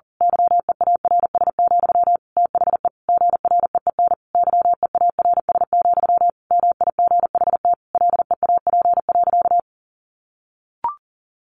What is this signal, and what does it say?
Morse code, heard as a keyed tone, about 40 wpm, 700 Hz. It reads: YEARS, THE GREEN YEARS, MIGHT LEAP.
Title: A Sound Of Thunder 40wpm